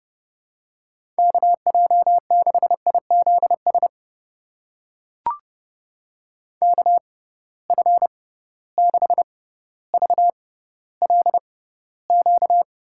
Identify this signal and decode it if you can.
Morse code, heard as a keyed tone, about 32 wpm, spaced as for 16 wpm, 700 Hz. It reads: KJ6IZS KF6VLQ
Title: USA Call Signs - 6 Characters - 6x Character Spacing - 30wpm